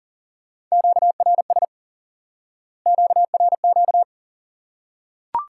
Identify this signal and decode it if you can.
Morse code, heard as a keyed tone, about 40 wpm, 700 Hz. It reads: QRS QRQ
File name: IQE Course 3 - Lesson 89 - YAGI - Two Elements - 40wpm